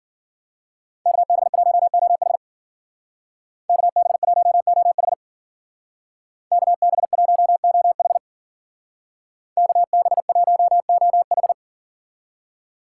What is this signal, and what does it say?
Morse code, heard as a keyed tone, about 48 wpm, 700 Hz. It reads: KB1OH KB1OH KB1OH KB1OH
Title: Speed Racing - USA 2x2 Call Signs 40wpm